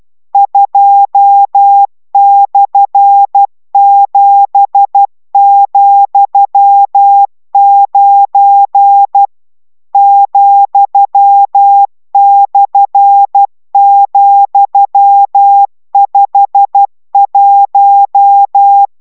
Morse code, heard as a keyed tone, about 12 wpm, 800 Hz. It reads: 2/7,9 ,/,51